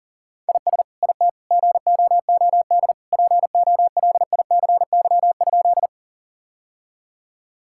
Morse code, heard as a keyed tone, about 40 wpm, 700 Hz. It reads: IS IT GOOD POLICY?
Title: Sentences from Top 1000 Words - Review Part 5 - 40wpm